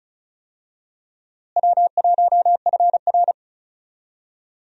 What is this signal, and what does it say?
Morse code, heard as a keyed tone, about 34 wpm, 700 Hz. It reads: W1FR
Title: USA Call Signs - 4 Characters - Repeated 3x - 35wpm